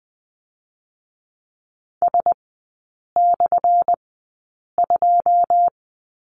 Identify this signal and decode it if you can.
Morse code, heard as a keyed tone, about 20 wpm, spaced as for 8 wpm, 700 Hz. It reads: S/2
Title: 205 Lesson 33e - Letter j - Three Characters 20wpm